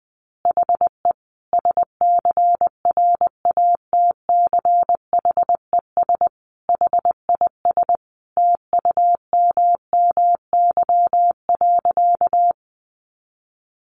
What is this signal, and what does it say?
Morse code, heard as a keyed tone, about 19 wpm, 700 Hz. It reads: HE SCRATCHES HIS TUMMY.
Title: Easy Phrases 20wpm